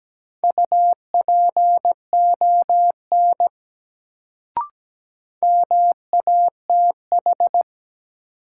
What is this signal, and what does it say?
Morse code, heard as a keyed tone, about 17 wpm, 700 Hz. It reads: UPON MATH
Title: Sentences from Top 1000 Words - Review Part 6 - New Word Drill - 17wpm